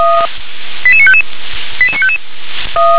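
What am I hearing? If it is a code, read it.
Five-tone selective call (ZVEI), heard as quick runs of five tones, tones of about 70 ms. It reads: 88955 88955
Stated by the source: ZVEI five-tone selective call received on SDR.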